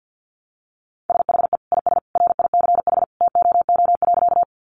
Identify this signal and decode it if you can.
Morse code, heard as a keyed tone, about 48 wpm, 700 Hz. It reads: SHE IS RICH TOO.
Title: Sentences from Top 1100 Words 50wpm